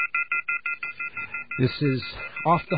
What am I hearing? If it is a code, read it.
Off-hook warning tone, 1400 + 2060 + 2450 + 2600 Hz, with speech.